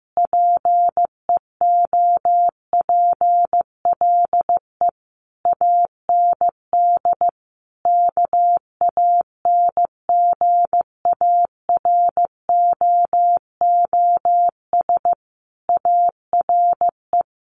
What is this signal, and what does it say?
Morse code, heard as a keyed tone, about 15 wpm, 700 Hz. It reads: PEOPLE AND KANGAROOS ARE
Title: Easy Phrases 15wpm